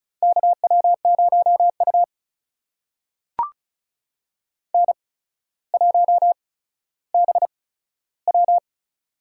Morse code, heard as a keyed tone, about 36 wpm, spaced as for 15 wpm, 700 Hz. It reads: KW0U N1BW
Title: USA Call Signs - 4 Characters - 8x Character Spacing - 35wpm